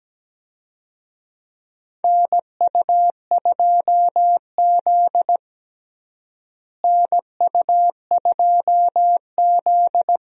Morse code, heard as a keyed tone, about 17 wpm, 700 Hz. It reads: NU2Z NU2Z
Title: USA Call Signs - 4 Characters - Repeated 3x - 17wpm